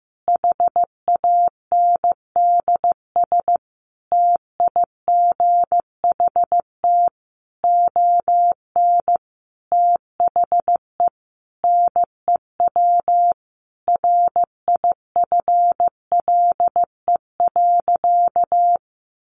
Morse code, heard as a keyed tone, about 15 wpm, 700 Hz. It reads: HANDS TIGHT ON THE NEW RIFLE.
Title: A Sound Of Thunder 15wpm